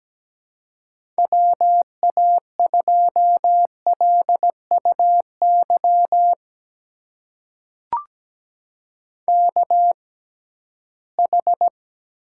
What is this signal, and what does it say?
Morse code, heard as a keyed tone, about 17 wpm, spaced as for 13 wpm, 700 Hz. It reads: WA2LUY KH
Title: USA Call Signs - 6 Characters - 6x Character Spacing - 17wpm